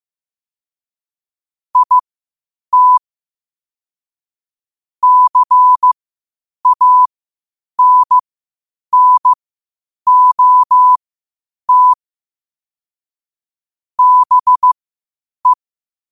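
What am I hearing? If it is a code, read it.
Morse code, heard as a keyed tone, about 15 wpm, spaced as for 8 wpm, 1000 Hz. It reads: IT CANNOT BE
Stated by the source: Morse practice at 8 WPM from Morse Resource.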